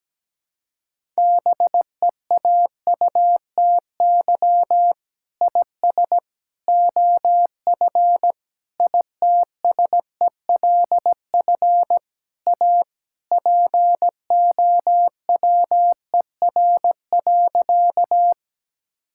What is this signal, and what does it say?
Morse code, heard as a keyed tone, about 17 wpm, 700 Hz. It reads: BEAUTY IS OF ITSELF A POWER.
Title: Sentences from Top 1000 Words - Review Part 5 - 17wpm